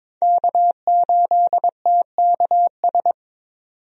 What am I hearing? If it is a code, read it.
Morse code, heard as a keyed tone, about 22 wpm, 700 Hz. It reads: K8TKS
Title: USA Call Signs - 5 Characters - 22wpm